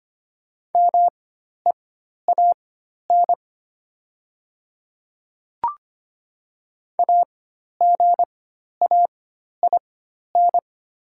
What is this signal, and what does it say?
Morse code, heard as a keyed tone, about 25 wpm, spaced as for 12 wpm, 700 Hz. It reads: MEAN AGAIN
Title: 4x Character Spacing - Top 200 Words - 25wpm